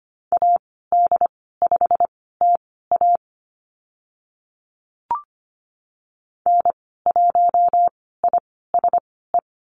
Morse code, heard as a keyed tone, about 25 wpm, spaced as for 16 wpm, 690 Hz. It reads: AD5TA N1ISE
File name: Lesson 15g - Number 5 - Call Signs 25wpm